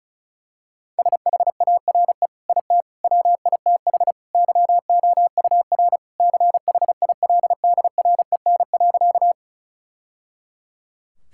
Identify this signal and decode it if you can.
Morse code, heard as a keyed tone, about 36 wpm, 700 Hz. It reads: SHARE IT WITH YOUR CHILDREN.
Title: Sentences from Top 1000 Words - Review Part 6 - 35wpm